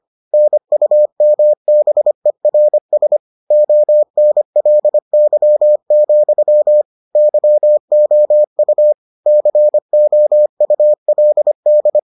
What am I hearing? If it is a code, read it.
Morse code, heard as a keyed tone, about 25 wpm, 600 Hz. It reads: NUMBERS ONLY, YOU COULD